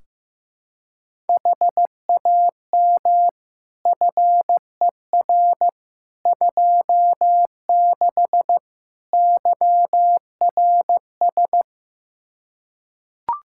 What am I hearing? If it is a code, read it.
Morse code, heard as a keyed tone, about 15 wpm, 700 Hz. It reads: HAM FER 26 YRS
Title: Practice QSO - Exchange 3 - More Info - 15wpm